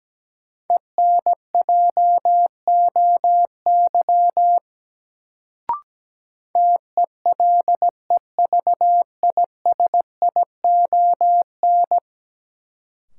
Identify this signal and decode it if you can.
Morse code, heard as a keyed tone, about 17 wpm, 700 Hz. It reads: ENJOY TELEVISION
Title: Sentences from Top 1000 Words - Review Part 4 - New Word Drill - 17wpm